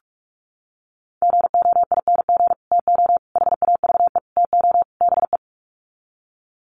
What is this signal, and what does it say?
Morse code, heard as a keyed tone, about 44 wpm, 700 Hz. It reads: GOING TO HAVE TO BE
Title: Sets of 5 Words 45wpm